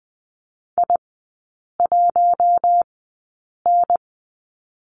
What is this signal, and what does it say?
Morse code, heard as a keyed tone, about 20 wpm, spaced as for 8 wpm, 700 Hz. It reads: I1N
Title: 041 Lesson 8e - Number 1 - Three Characters 20wpm